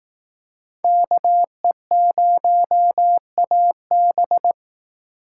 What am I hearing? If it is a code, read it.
Morse code, heard as a keyed tone, about 18 wpm, 700 Hz. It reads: KE0AB
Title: Speed Racing - USA 2x2 Call Signs 15wpm